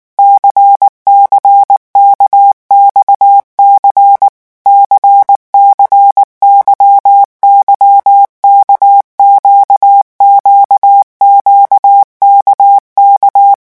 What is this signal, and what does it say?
Morse code, heard as a keyed tone, about 19 wpm, 800 Hz. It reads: CCKXC CCYYK QQQKK
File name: revision1-vitesse 18 mots